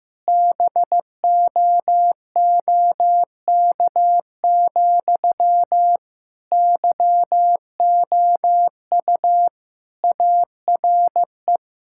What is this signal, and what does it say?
Morse code, heard as a keyed tone, about 15 wpm, 700 Hz. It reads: BOOK, YOU ARE
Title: Sentences from Top 1000 Words - Review Part 5 - 15wpm